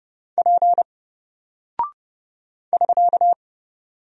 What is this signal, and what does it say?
Morse code, heard as a keyed tone, about 30 wpm, 700 Hz. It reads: P <SK>
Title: Single Character (Letter:Number:Punctuation:Prosign) Alt 30wpm